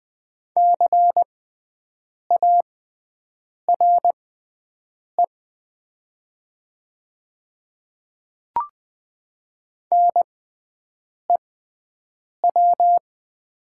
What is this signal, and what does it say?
Morse code, heard as a keyed tone, about 20 wpm, spaced as for 7 wpm, 690 Hz. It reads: CARE NEW
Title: 6x Character Spacing - Top 300 Words - 20wpm